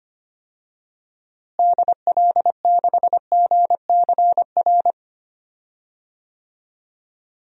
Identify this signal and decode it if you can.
Morse code, heard as a keyed tone, about 25 wpm, 700 Hz. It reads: DL6GCR
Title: DXCC-100 Call Signs - 6 Characters - Repeated 3x - 25wpm